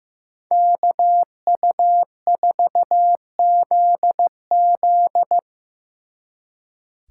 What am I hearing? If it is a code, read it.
Morse code, heard as a keyed tone, about 18 wpm, 700 Hz. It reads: KU4ZZ
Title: Speed Racing - USA Call Signs 15wpm